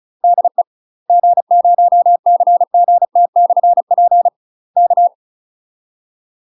Morse code, heard as a keyed tone, about 36 wpm, 700 Hz. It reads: DE G0CGT/P K
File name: UK MASTER QSOs TRAINING/35WPM